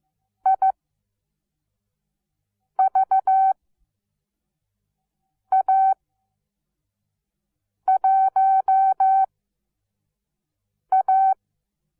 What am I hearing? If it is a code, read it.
Morse code, heard as a keyed tone, about 15 wpm, spaced as for 4 wpm, 770 Hz. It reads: IVA1A
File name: Onlinelehrgang / morsen / DJ4UF Morsen17